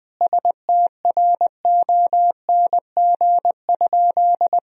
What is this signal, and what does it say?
Morse code, heard as a keyed tone, about 20 wpm, 700 Hz. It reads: STRONG?
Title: Sentences from Top 1000 Words - Review Part 3 - 20wpm